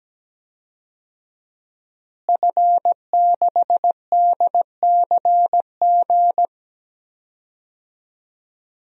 Morse code, heard as a keyed tone, about 17 wpm, 700 Hz. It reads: F6DCG
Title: DXCC-100 Call Signs - 5 Characters - Repeated 2x - 17wpm